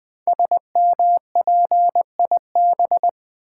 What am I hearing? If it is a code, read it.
Morse code, heard as a keyed tone, about 20 wpm, 700 Hz. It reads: SMPIB
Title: Code Groups - 5 Letters - 20wpm